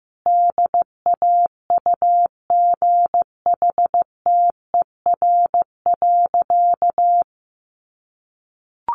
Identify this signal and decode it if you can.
Morse code, heard as a keyed tone, about 14 wpm, 700 Hz. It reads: DAUGHTER.
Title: Sentences from Top 1500 Words 15wpm